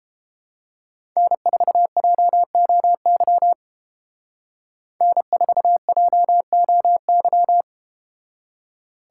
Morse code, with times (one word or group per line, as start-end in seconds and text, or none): 1.2-3.5 N4JOY
5.0-7.6 N4JOY